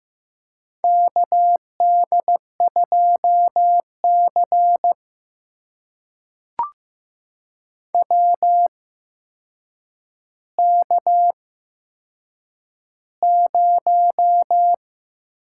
Morse code, heard as a keyed tone, about 15 wpm, spaced as for 7 wpm, 700 Hz. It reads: KD2C WK0
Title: USA Call Signs - 4 Characters - 8x Character Spacing - 15wpm